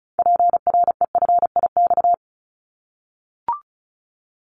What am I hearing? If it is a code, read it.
Morse code, heard as a keyed tone, about 34 wpm, 700 Hz. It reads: PREFIX
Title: Callsign Prefixes for Top 25 DXCC - 35wpm